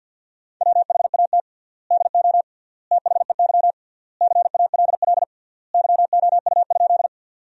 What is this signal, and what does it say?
Morse code, heard as a keyed tone, about 50 wpm, 700 Hz. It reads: WHAT DO THEY CALL YOU?
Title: 3x Word-Spacing - Sentences from Top 100 Words - 50wpm